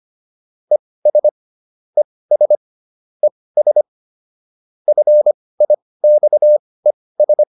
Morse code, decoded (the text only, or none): ES ES ES FIXES